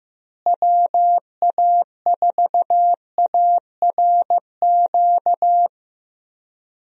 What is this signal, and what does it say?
Morse code, heard as a keyed tone, about 15 wpm, 700 Hz. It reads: WA4ARQ
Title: USA Call Signs - 6 Characters - Repeated 3x - 15wpm